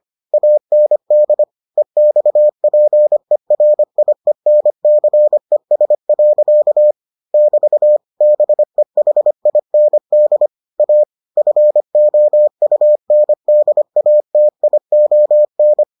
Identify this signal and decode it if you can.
Morse code, heard as a keyed tone, about 25 wpm, 600 Hz. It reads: AND EXPERIENCES. <BT> BEHIND A FOUNDATION